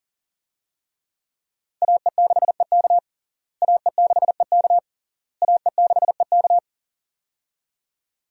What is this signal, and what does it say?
Morse code, heard as a keyed tone, about 40 wpm, 700 Hz. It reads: AE6EK AE6EK AE6EK
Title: USA Call Signs - 5 Characters - Repeated 3x - 40wpm